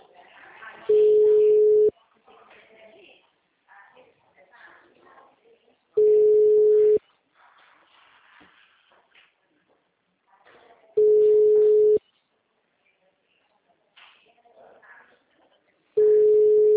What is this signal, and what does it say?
Ringback tone (Europe, 425 Hz), with speech.
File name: out- -2008- - - .12523